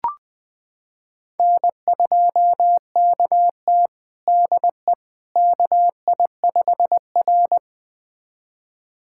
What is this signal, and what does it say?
Morse code, heard as a keyed tone, about 21 wpm, 700 Hz. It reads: N2KT DE KI5R
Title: USA Callsign DE Callsign - 4 Characters - 20wpm